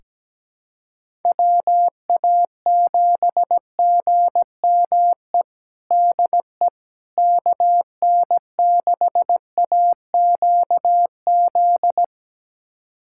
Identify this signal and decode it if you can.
Morse code, heard as a keyed tone, about 17 wpm, 700 Hz. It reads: WA7GME DE KN6AQZ
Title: USA Callsign DE Callsign 17wpm